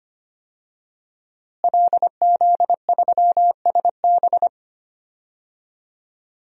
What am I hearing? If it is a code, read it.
Morse code, heard as a keyed tone, about 25 wpm, 700 Hz. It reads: LZ3SB
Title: DXCC-100 Call Signs - 5 Characters - Repeated 3x - 25wpm